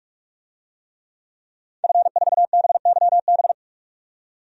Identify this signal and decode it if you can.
Morse code, heard as a keyed tone, about 47 wpm, 700 Hz. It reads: F4BYB
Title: DXCC-100 Call Signs - 5 Characters - 45wpm